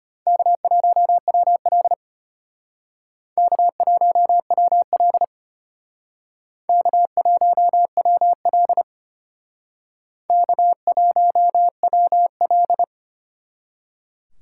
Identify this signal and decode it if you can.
Morse code, heard as a keyed tone, about 33 wpm, 700 Hz. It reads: K1WL K1WL K1WL K1WL